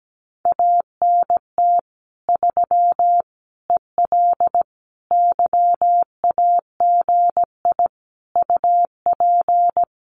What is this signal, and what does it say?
Morse code, heard as a keyed tone, about 17 wpm, 700 Hz. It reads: ANT 3 EL YAGI UP
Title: Common QSO Phrases Encore 17wpm